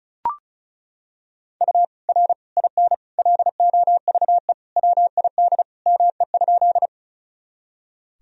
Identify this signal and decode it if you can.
Morse code, heard as a keyed tone, about 34 wpm, 700 Hz. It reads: U R IN LOVE WID ME?
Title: Sentences from Top 400 Words and Abbreviations 35wpm